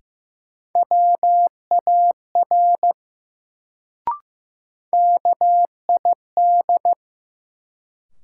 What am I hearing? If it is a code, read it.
Morse code, heard as a keyed tone, about 15 wpm, 700 Hz. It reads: WAR KID
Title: Sets of 5 Words - New Word Drill - 15wpm